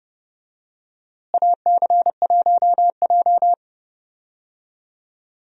Morse code, heard as a keyed tone, about 30 wpm, 700 Hz. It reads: AC1J
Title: USA Call Signs - 4 Characters - Repeated 3x - 30wpm